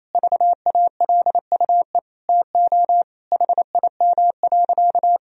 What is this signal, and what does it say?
Morse code, heard as a keyed tone, about 28 wpm, 700 Hz. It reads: VALUE TO HIM.
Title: Sentences from Top 1000 Words - Review Part 3 - 28wpm